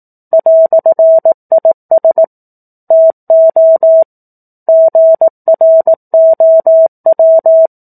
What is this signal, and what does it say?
Morse code, heard as a keyed tone, about 18 wpm, 650 Hz. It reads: "IS TO GROW
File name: 18 WPM